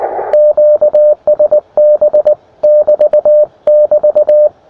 Morse code, heard as a keyed tone, about 19 wpm, 600 Hz, with radio noise. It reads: QSB <BT><BT>
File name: attachments / us7ign a 20m